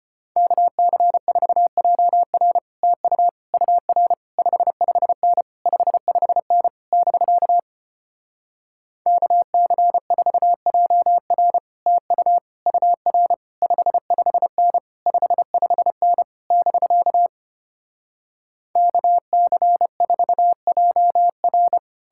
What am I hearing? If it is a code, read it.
Morse code, heard as a keyed tone, about 33 wpm, 700 Hz. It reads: KC4JR TU UR 55N 55N <BK> KC4JR TU UR 55N 55N <BK> KC4JR